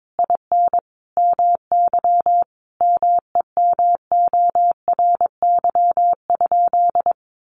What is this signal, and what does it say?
Morse code, heard as a keyed tone, about 22 wpm, 700 Hz. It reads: IN MY MEMORY?
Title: Sentences from Top 1300 Words 22wpm